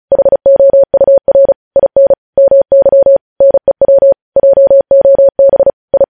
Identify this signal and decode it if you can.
Morse code, heard as a keyed tone, about 36 wpm, 550 Hz. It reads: HOUR IN MY NEW JOB I
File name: podcast / News headlines-1x-35x35